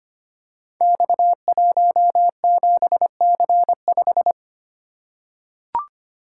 Morse code, heard as a keyed tone, about 25 wpm, 700 Hz. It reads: X17C5
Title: Code Groups - 5 Letter:Numbers - 25wpm